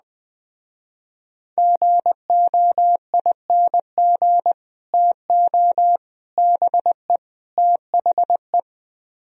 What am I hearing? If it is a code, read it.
Morse code, heard as a keyed tone, about 20 wpm, 700 Hz. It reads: GOING TO BE THE
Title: Speed-Racing - Sets of 4 Words 17wpm